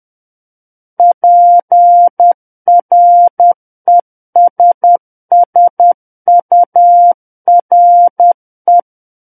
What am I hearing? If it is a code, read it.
Morse code, heard as a keyed tone, about 10 wpm, 700 Hz. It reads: PRESSURE
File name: Book One / 01-Chapter 1 - 10 WPM